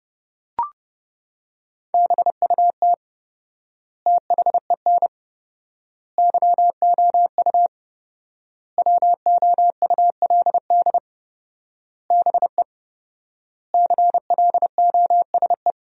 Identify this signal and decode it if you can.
Morse code, heard as a keyed tone, about 30 wpm, 700 Hz. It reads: BUT THEN YOU WOULD BE CLOSE
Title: 4x Word-Spacing - Sentences from Top 300 Words - 30wpm